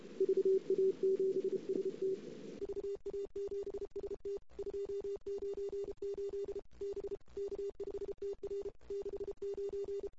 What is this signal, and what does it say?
Morse code, heard as a keyed tone, about 32 wpm, 410 Hz. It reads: VA7ST VA7ST 298 B K5TR 69
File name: CW / 2008-ss-cw-k5tr-op-n5rz -23:00